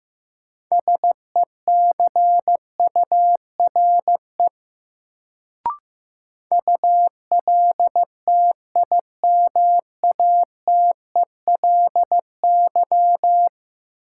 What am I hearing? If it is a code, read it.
Morse code, heard as a keyed tone, about 15 wpm, 700 Hz. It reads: SECURE ULTIMATELY
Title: Sentences from Top 1200 Words - New Word Drill - 15wpm